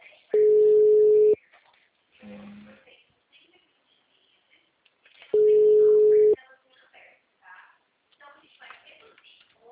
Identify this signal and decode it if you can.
Ringback tone (Europe, 425 Hz), with speech.